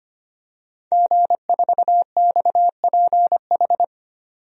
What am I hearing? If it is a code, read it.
Morse code, heard as a keyed tone, about 25 wpm, 700 Hz. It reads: G4XPH
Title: DXCC-100 Call Signs - 5 Characters - 25wpm